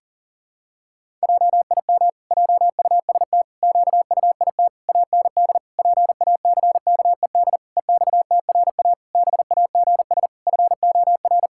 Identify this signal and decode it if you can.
Morse code, heard as a keyed tone, about 40 wpm, 700 Hz. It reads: JIM JUST QUIT AND PACKED EXTRA BAGS FOR
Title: Pangram Challenge v2 - 40wpm